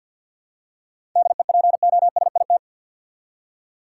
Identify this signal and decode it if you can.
Morse code, heard as a keyed tone, about 48 wpm, 700 Hz. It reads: DEPOSIT
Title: Seven Letter Words - 50wpm